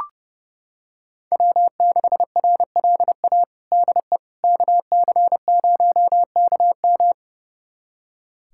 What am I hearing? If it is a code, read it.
Morse code, heard as a keyed tone, about 32 wpm, 700 Hz. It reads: W6RLA DE KC0KM
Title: USA Callsign DE Callsign - 5 Characters - 30wpm